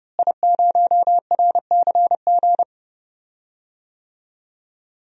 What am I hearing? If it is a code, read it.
Morse code, heard as a keyed tone, about 30 wpm, 700 Hz. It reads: I0RCG
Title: DXCC-100 Call Signs - 5 Characters - Repeated 2x - 30wpm